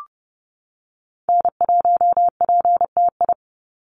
Morse code, heard as a keyed tone, about 30 wpm, 700 Hz. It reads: N1PTI
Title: USA Call Signs 30wpm